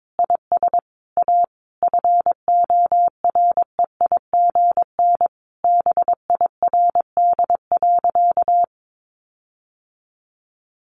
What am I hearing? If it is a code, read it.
Morse code, heard as a keyed tone, about 22 wpm, 700 Hz. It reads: IS A FOREIGN BIRD.
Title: Sentences from Top 1400 Words 22wpm